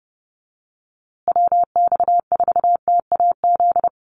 Morse code, heard as a keyed tone, about 29 wpm, 700 Hz. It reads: WX4TAZ
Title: USA Call Signs Repeated 2x 30wpm